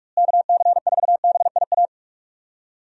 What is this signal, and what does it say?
Morse code, heard as a keyed tone, about 47 wpm, 700 Hz. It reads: KC4BIA
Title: USA Call Signs - 6 Characters - Repeated 2x - 45wpm